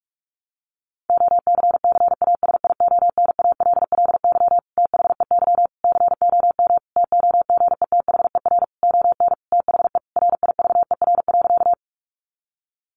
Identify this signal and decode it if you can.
Morse code, heard as a keyed tone, about 44 wpm, 700 Hz. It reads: OCCASIONALLY THEY COM TOGETHER ON THE RIVER.
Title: Sentences from Top 1300 Words 45wpm